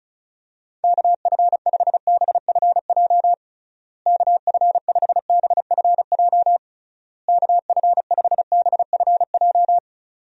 Morse code, heard as a keyed tone, about 36 wpm, 700 Hz. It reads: KF5BFJ KF5BFJ KF5BFJ
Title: USA Call Signs - 6 Characters - Repeated 3x - 35wpm